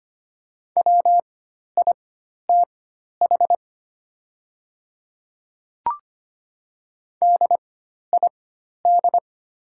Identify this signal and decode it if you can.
Morse code, heard as a keyed tone, about 25 wpm, spaced as for 12 wpm, 700 Hz. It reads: WITH DID
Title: 4x Character Spacing - Top 100 Words - 25wpm